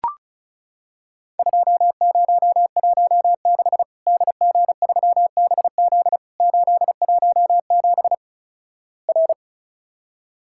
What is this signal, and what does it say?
Morse code, heard as a keyed tone, about 36 wpm, 700 Hz. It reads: 2016 DG3BZ 817 R
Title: Worked All Europe DX Contest - QTC Exchange - 35wpm